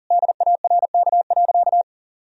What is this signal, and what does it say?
Morse code, heard as a keyed tone, about 40 wpm, 700 Hz. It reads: DARK.
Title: Sentences from Top 1000 Words - Review Part 4 - 40wpm